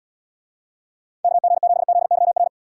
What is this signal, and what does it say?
Morse code, heard as a keyed tone, about 78 wpm, 700 Hz. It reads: KG7ZYR